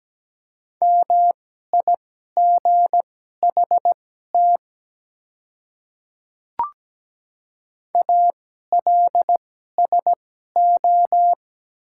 Morse code, heard as a keyed tone, about 17 wpm, spaced as for 12 wpm, 700 Hz. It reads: MIGHT ALSO
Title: 2x Character Spacing - Top 300 Words - 17wpm